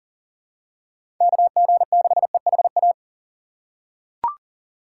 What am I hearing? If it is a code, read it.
Morse code, heard as a keyed tone, about 40 wpm, 700 Hz. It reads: KG6EHA
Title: USA Call Signs - 6 Characters - 2x Character Spacing - 40wpm